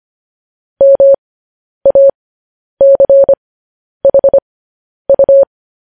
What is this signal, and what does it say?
Morse code, heard as a keyed tone, about 25 wpm, spaced as for 10 wpm, 550 Hz. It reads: MACHU
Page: News Headlines in Morse Code at 10 WPM